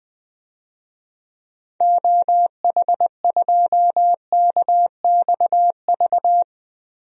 Morse code, heard as a keyed tone, about 20 wpm, 700 Hz. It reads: OH2KXV
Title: DXCC-100 Call Signs - 6 Characters - 20wpm